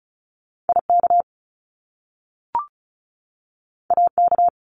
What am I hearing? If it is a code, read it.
Morse code, heard as a keyed tone, about 36 wpm, 700 Hz. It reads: IK AK
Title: Top Two Letter Combinations in Words 35wpm